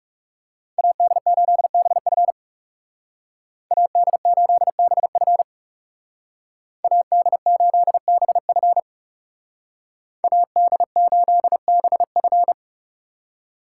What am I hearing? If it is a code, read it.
Morse code, heard as a keyed tone, about 35 wpm, 700 Hz. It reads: AD8BF AD8BF AD8BF AD8BF
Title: Speed Racing - USA 2x2 Call Signs 30wpm